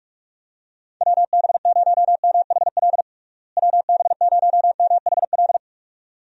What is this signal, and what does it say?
Morse code, heard as a keyed tone, about 47 wpm, 700 Hz. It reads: WB0MHL WB0MHL
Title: USA Call Signs - 6 Characters - Repeated 2x - 45wpm